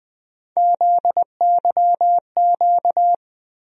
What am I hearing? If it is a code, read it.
Morse code, heard as a keyed tone, about 20 wpm, 700 Hz. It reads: ZYQ
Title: USA Call Sign Postfixes - 3 Letters 20wpm